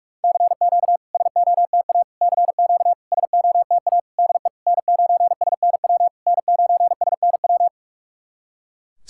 Morse code, heard as a keyed tone, about 44 wpm, 690 Hz. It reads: CQ SOTA CQ SOTA DE N9SNW N9SNW
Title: SOTA QSO Practice 45wpm